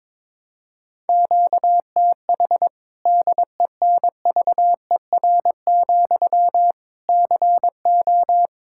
Morse code, heard as a keyed tone, about 22 wpm, 700 Hz. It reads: QTH DENVER, CO
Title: Practice QSO - Exchange 1 - Beginning - 22wpm